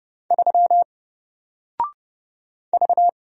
Morse code, heard as a keyed tone, about 29 wpm, 700 Hz. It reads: 3 V
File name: Single-Letter-Number-30wpm